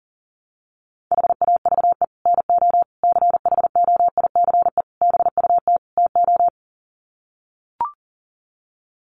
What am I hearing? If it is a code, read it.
Morse code, heard as a keyed tone, about 40 wpm, 700 Hz. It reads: HAVE NO CHOICE BUT TO
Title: Sets of 5 Words Encore 40wpm